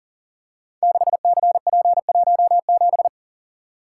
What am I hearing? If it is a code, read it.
Morse code, heard as a keyed tone, about 42 wpm, 700 Hz. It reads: 6CP17
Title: Code Groups - 5 Letter:Numbers - 40wpm